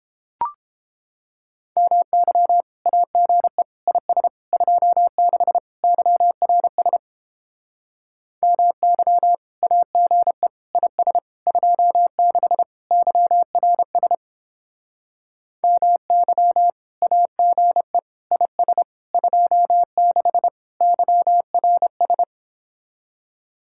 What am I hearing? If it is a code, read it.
Morse code, heard as a keyed tone, about 31 wpm, 700 Hz. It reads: MY AGE IS 26 YRS MY AGE IS 26 YRS MY AGE IS 26 YRS